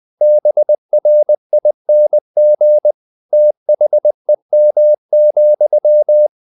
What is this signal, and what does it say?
Morse code, heard as a keyed tone, about 20 wpm, 600 Hz. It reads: BRING THEM,
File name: the old man and the sea part 2 20wpm